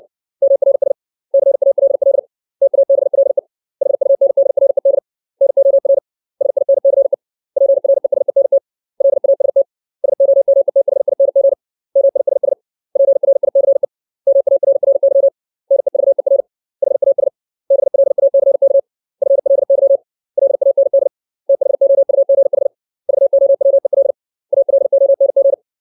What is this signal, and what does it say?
Morse code, heard as a keyed tone, about 64 wpm, 550 Hz. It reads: MRS CALL TABLE HUNDRED NOR SINCE PRESENT LAST SOMETHING MISS PRICE MARRY NEVER HAS BLACK FLY LAND THOUGH VOWEL ALONG
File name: 632 words / 632 Words 60WPM